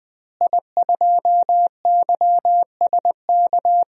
Morse code, heard as a keyed tone, about 21 wpm, 700 Hz. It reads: I2YSK
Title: DXCC-100 Call Signs - 5 Characters - Repeated 2x - 20wpm